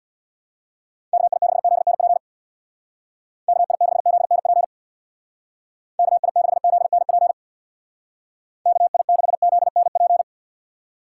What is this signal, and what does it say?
Morse code, heard as a keyed tone, about 65 wpm, 700 Hz. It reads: KI6ZNP KI6ZNP KI6ZNP KI6ZNP